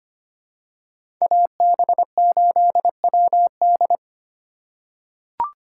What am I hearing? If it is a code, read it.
Morse code, heard as a keyed tone, about 25 wpm, 700 Hz. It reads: AB8WD
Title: USA Call Signs - 5 Characters - 2x Character Spacing - 25wpm